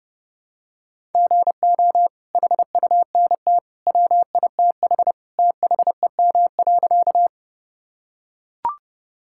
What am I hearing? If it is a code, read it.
Morse code, heard as a keyed tone, about 32 wpm, 700 Hz. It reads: GO HUNT WITH THEM.
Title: Sentences from Top 1600 Words 30wpm